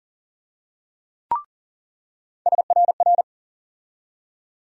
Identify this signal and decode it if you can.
Morse code, heard as a keyed tone, about 40 wpm, 700 Hz. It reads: SRR
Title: USA Call Sign Postfixes - 3 Letters 40wpm